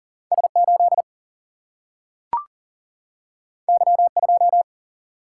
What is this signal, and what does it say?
Morse code, heard as a keyed tone, about 38 wpm, 700 Hz. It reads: S 8 Y 2
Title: Single-Letter Single-Number 40wpm